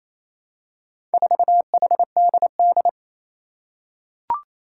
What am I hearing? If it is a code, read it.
Morse code, heard as a keyed tone, about 28 wpm, 700 Hz. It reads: 4HDD
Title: Four Challenging Letter-Number Combinations 28wpm